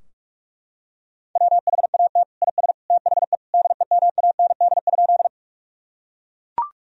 Morse code, heard as a keyed tone, about 41 wpm, 700 Hz. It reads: WHAT IS THE DEMAND?
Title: Sentences from Top 1000 Words - Review Part 4 - 45wpm